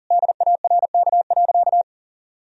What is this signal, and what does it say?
Morse code, heard as a keyed tone, about 40 wpm, 700 Hz. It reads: DARK.
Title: Sentences from Top 1000 Words - Review Part 4 - 40wpm